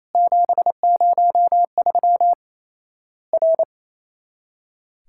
Morse code, heard as a keyed tone, about 28 wpm, 700 Hz. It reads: 703 R
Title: Worked All Europe DX Contest - QTC Exchange - 28wpm